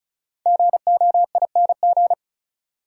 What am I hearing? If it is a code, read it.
Morse code, heard as a keyed tone, about 36 wpm, 700 Hz. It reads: GOING
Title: Sentences from Top 500 Words - New Word Drill - 35wpm